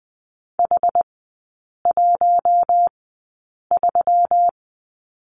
Morse code, heard as a keyed tone, about 20 wpm, spaced as for 8 wpm, 700 Hz. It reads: H13
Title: 172 Lesson 28e - Number 9 - Three Characters 20wpm